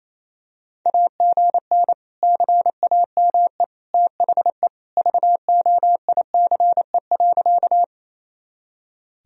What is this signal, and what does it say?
Morse code, heard as a keyed tone, about 28 wpm, 700 Hz. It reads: AGN CAME THE VOICE.
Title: Sentences from Top 500 Words and Abbreviations 28wpm